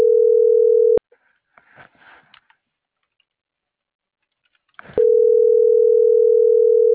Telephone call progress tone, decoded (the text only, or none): ringback tone